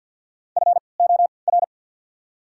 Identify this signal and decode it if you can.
Morse code, heard as a keyed tone, about 48 wpm, 700 Hz. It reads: F O R
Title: 3x Character Spacing - Top 300 Words - 50wpm